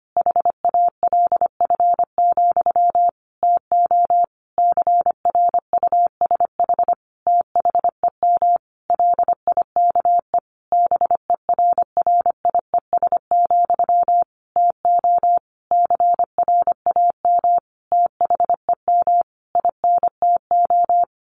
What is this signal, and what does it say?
Morse code, heard as a keyed tone, about 25 wpm, 700 Hz. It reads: HALF, TO CRUSH THEM LIKE BERRIES, TO CRAM THEM INTO
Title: A Sound Of Thunder 25wpm